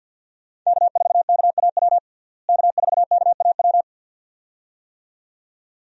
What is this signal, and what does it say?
Morse code, heard as a keyed tone, about 53 wpm, 700 Hz. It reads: K4KAW K4KAW
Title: USA Call Signs - 5 Characters - Repeated 2x - 50wpm